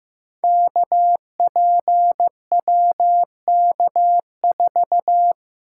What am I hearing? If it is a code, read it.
Morse code, heard as a keyed tone, about 15 wpm, 700 Hz. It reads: KPWK4
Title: Code Groups - 5 Letter:Numbers - 15wpm